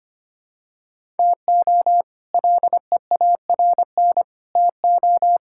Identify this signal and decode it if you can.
Morse code, heard as a keyed tone, about 25 wpm, 700 Hz. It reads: TO LEARN TO
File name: practice / 25wpm / All-Age Appropriate Jokes - 25wpm